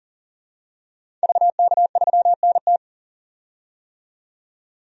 Morse code, heard as a keyed tone, about 40 wpm, 700 Hz. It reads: VK3NT
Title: DXCC-100 Call Signs - 5 Characters - 40wpm